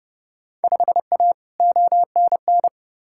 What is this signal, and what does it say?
Morse code, heard as a keyed tone, about 30 wpm, 700 Hz. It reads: 5A ONN
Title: Field Day Exchanges Encore - 30wpm